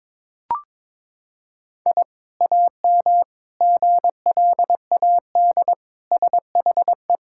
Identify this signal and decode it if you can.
Morse code, heard as a keyed tone, about 22 wpm, 700 Hz. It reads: I AM GLAD SHE
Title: Sentences from Top 1000 Words - Review Part 6 - 22wpm